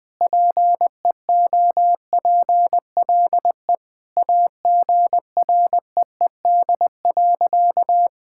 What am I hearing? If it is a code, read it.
Morse code, heard as a keyed tone, about 20 wpm, 700 Hz. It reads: PEOPLE AGREED.
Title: Sentences with 5-Word Phrases - Spoken Beginning - 20wpm